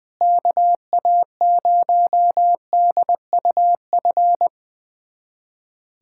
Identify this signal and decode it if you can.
Morse code, heard as a keyed tone, about 21 wpm, 700 Hz. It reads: KA0DUF
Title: USA Callsign DE Callsign 20wpm